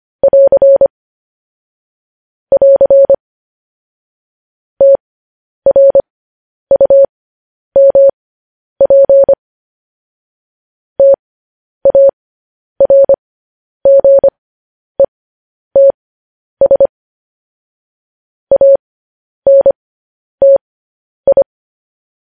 Morse code, heard as a keyed tone, about 25 wpm, spaced as for 10 wpm, 550 Hz. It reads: <AR> <AR> TRUMP TARGETS ANTI
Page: news-headlines-in-morse-code-at-10-wpm